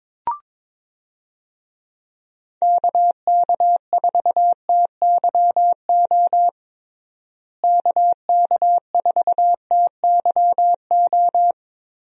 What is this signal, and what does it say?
Morse code, heard as a keyed tone, about 22 wpm, 700 Hz. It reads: KK4TYO KK4TYO
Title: USA Call Signs - 6 Characters - Repeated 3x - 22wpm